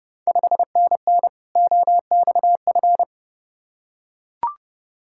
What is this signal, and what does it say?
Morse code, heard as a keyed tone, about 30 wpm, 700 Hz. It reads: 5NN OXF
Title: Ontario QSO Party Exchange 30wpm